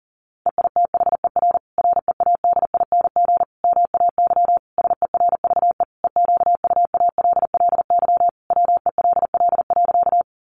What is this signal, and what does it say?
Morse code, heard as a keyed tone, about 40 wpm, 700 Hz. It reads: EITHER READING MAY SERVE EQUALLY WELL.
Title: Sentences from Top 1400 Words 40wpm